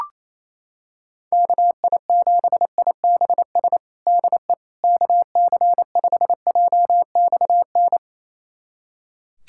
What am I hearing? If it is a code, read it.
Morse code, heard as a keyed tone, about 28 wpm, 700 Hz. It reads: KI7IBS DE KC5JXN
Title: USA Callsign DE Callsign 28wpm